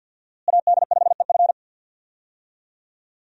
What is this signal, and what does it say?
Morse code, heard as a keyed tone, about 53 wpm, 700 Hz. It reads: AD5EF
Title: USA Call Signs - 5 Characters - Repeated 2x - 50wpm